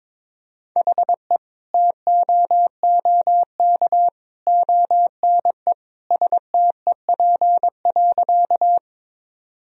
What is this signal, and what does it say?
Morse code, heard as a keyed tone, about 22 wpm, 700 Hz. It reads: HE TOOK ONE STEP.
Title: Sentences from Top 1000 Words - Review Part 2 - 22wpm